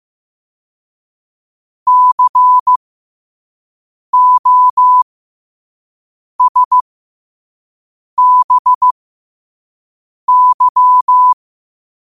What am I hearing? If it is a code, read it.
Morse code, heard as a keyed tone, about 15 wpm, spaced as for 5 wpm, 1000 Hz. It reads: COSBY
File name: qotdfiles / 5wpm